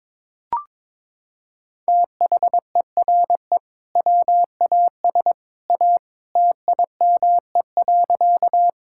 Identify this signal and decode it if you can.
Morse code, heard as a keyed tone, about 22 wpm, 700 Hz. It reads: THERE WAS A TIME.
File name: practice / 22wpm / Sentences with 2-Word Phrases - 22wpm